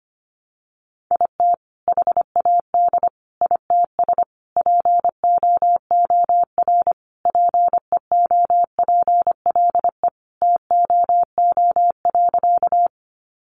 Morse code, heard as a keyed tone, about 25 wpm, 700 Hz. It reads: IT HAD ITS POOR PEOPLE TOO.
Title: Sentences from Top 1000 Words 25wpm